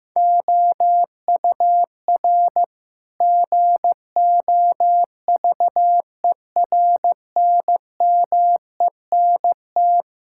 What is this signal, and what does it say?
Morse code, heard as a keyed tone, about 15 wpm, 700 Hz. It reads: OUR GOVERNMENT
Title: Sentences from Top 1000 Words - Review Part 5 - 15wpm